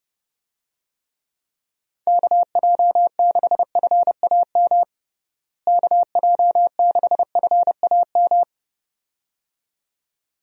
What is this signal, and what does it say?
Morse code, heard as a keyed tone, about 32 wpm, 700 Hz. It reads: KJ6FAM KJ6FAM
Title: USA Call Signs - 6 Characters - Repeated 2x - 30wpm